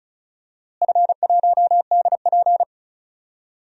Morse code, heard as a keyed tone, about 36 wpm, 700 Hz. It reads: F1DP
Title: DXCC-100 Call Signs - 4 Characters - Repeated 3x - 35wpm